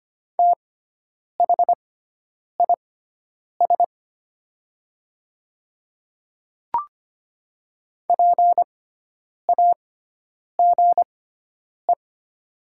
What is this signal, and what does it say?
Morse code, heard as a keyed tone, about 25 wpm, spaced as for 9 wpm, 700 Hz. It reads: THIS PAGE
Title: 6x Character Spacing - Top 300 Words - 25wpm